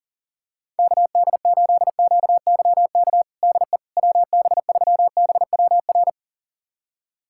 Morse code, heard as a keyed tone, about 40 wpm, 700 Hz. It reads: KD8QYK DE WB3BWR
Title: USA Callsign DE Callsign 40wpm